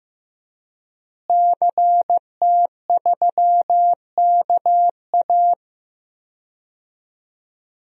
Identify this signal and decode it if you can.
Morse code, heard as a keyed tone, about 15 wpm, 700 Hz. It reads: CT3KA
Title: DXCC-100 Call Signs - 5 Characters - Repeated 3x - 15wpm